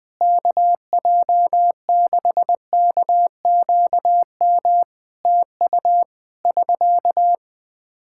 Morse code, heard as a keyed tone, about 20 wpm, 700 Hz. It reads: KJ6KQM TU <SK>
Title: Practice QSO - Exchange 4 - Ending - 20wpm